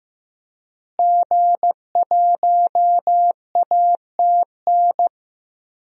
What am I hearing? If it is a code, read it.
Morse code, heard as a keyed tone, about 16 wpm, 700 Hz. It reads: G1ATN
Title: DXCC-100 Call Signs - 5 Characters - Repeated 3x - 15wpm